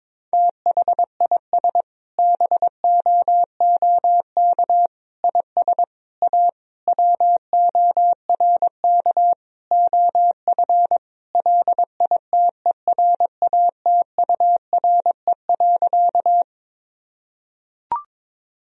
Morse code, heard as a keyed tone, about 22 wpm, 700 Hz. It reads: THIS BOOK IS A WORK OF LITERATURE.
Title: Sentences from Top 1000 Words - Review Part 3 - 22wpm